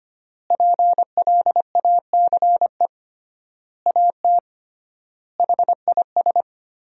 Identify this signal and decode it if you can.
Morse code, heard as a keyed tone, about 25 wpm, 700 Hz. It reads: PLACE AT HIS
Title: 3x Word-Spacing - Sentences from Top 300 Words - 25wpm